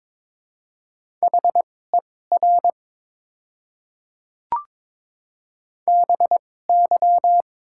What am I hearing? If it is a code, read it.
Morse code, heard as a keyed tone, about 22 wpm, spaced as for 16 wpm, 700 Hz. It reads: HER BY
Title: 2x Character Spacing - Top 200 Words - 22wpm